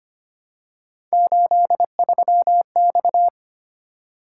Morse code, heard as a keyed tone, about 25 wpm, 700 Hz. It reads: 83X
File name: Three Letter-Number Combinations - 25wpm